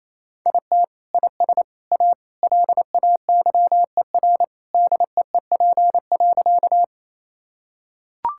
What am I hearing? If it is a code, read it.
Morse code, heard as a keyed tone, about 28 wpm, 700 Hz. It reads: IT IS A LAYER DEEP.
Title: Sentences from Top 1600 Words 28wpm